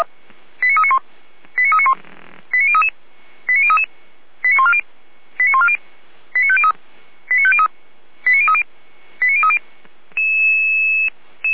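Five-tone selective call (ZVEI), heard as quick runs of five tones, tones of about 70 ms. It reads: 89391 89391 89033 89033 89140 89140 89503 89503 89030 89030
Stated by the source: ZVEI five-tone call received by SDR.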